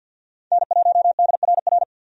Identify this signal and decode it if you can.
Morse code, heard as a keyed tone, about 50 wpm, 700 Hz. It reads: N1DRR
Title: USA Call Signs - 5 Characters - 50wpm